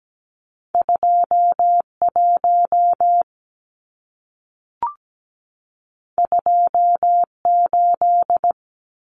Morse code, heard as a keyed tone, about 17 wpm, 700 Hz. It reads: 21 28
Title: Numbers - 2-Digit 17wpm